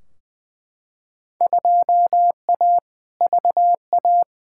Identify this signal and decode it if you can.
Morse code, heard as a keyed tone, about 20 wpm, 700 Hz. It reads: 2A VA
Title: Field Day Exchanges - 20wpm